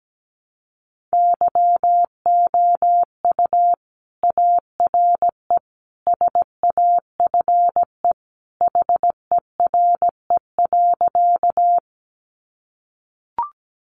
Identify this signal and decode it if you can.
Morse code, heard as a keyed tone, about 17 wpm, 700 Hz. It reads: YOU ARE SAFE HERE.
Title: Sentences from Top 900 Words 17wpm